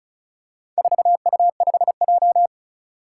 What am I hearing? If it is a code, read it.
Morse code, heard as a keyed tone, about 34 wpm, 700 Hz. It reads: 4U5J
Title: Four Challenging Letter-Number Combinations 35wpm